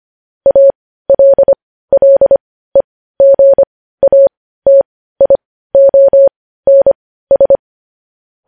Morse code, heard as a keyed tone, about 25 wpm, spaced as for 15 wpm, 550 Hz. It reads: ALLEGATIONS
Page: news-headlines-in-morse-code-at-15-wpm